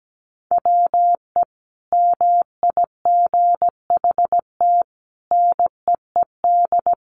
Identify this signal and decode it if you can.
Morse code, heard as a keyed tone, about 17 wpm, 700 Hz. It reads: WE MIGHT NEED
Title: A Sound Of Thunder 17wpm